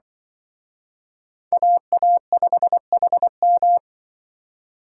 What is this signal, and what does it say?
Morse code, heard as a keyed tone, about 24 wpm, 700 Hz. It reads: AA5HM
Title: Speed Racing - USA 2x2 Call Signs 20wpm